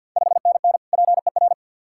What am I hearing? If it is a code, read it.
Morse code, heard as a keyed tone, about 46 wpm, 690 Hz. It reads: 5NN PER
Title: Ontario QSO Party Exchange 50wpm